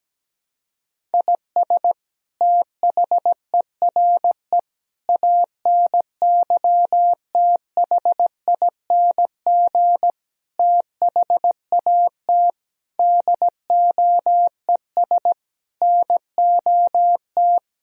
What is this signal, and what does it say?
Morse code, heard as a keyed tone, about 17 wpm, 700 Hz. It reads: IS THERE ANYTHING THAT DOES NOT
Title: Sentences from Top 1000 Words - Review Part 1 - 17wpm